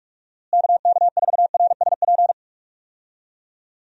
Morse code, heard as a keyed tone, about 41 wpm, 700 Hz. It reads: KK4RSP
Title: USA Call Signs - 6 Characters - Repeated 2x - 45wpm